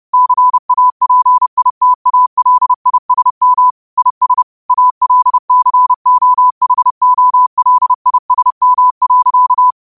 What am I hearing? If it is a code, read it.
Morse code, heard as a keyed tone, about 30 wpm, 1000 Hz. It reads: CAPITALISM IS ALCOHOLISM.